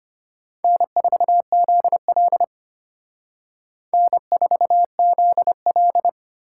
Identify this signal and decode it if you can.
Morse code, heard as a keyed tone, about 32 wpm, 700 Hz. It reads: N4ZL N4ZL